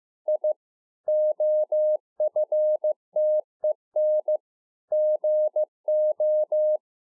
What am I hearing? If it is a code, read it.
Morse code, heard as a keyed tone, about 15 wpm, 610 Hz. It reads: I OFTEN GO